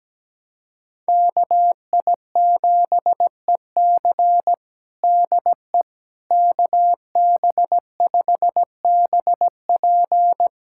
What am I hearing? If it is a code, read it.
Morse code, heard as a keyed tone, about 17 wpm, 700 Hz. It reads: KI7EC DE KB5BP
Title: USA Callsign DE Callsign - 5 Characters - 17wpm